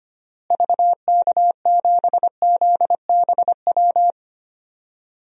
Morse code, heard as a keyed tone, about 25 wpm, 700 Hz. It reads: VK7ZBW
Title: DXCC-100 Call Signs - 6 Characters - Repeated 2x - 25wpm